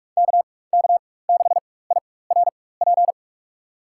Morse code, heard as a keyed tone, about 46 wpm, spaced as for 21 wpm, 700 Hz. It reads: KK6IRP
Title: USA Call Signs - 6 Characters - 4x Character Spacing - 45wpm